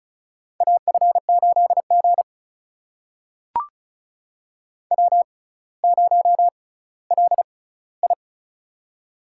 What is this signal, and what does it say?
Morse code, heard as a keyed tone, about 36 wpm, spaced as for 18 wpm, 700 Hz. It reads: AF8G W0LI
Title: USA Call Signs - 4 Characters - 6x Character Spacing - 35wpm